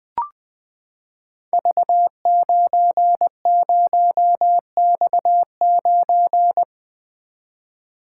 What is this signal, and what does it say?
Morse code, heard as a keyed tone, about 20 wpm, 700 Hz. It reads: V90X9
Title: Code Groups - 5 Letter:Numbers - 20wpm